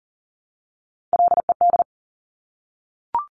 Morse code, heard as a keyed tone, about 41 wpm, 700 Hz. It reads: LED
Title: Top Three Letter Combinations in Words 40wpm